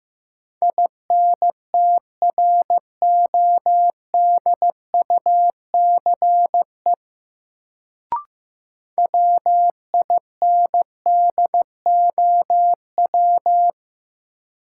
Morse code, heard as a keyed tone, about 15 wpm, 700 Hz. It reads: INTRODUCE WINDOW
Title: Sentences from Top 1600 Words - New Word Drill Alt - 15wpm